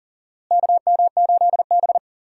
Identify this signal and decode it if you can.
Morse code, heard as a keyed tone, about 40 wpm, 700 Hz. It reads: KM8B
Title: USA Call Signs - 4 Characters - Repeated 2x - 40wpm